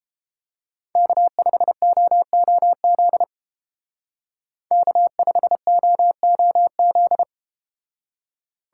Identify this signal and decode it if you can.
Morse code, heard as a keyed tone, about 33 wpm, 700 Hz. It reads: K5OOZ K5OOZ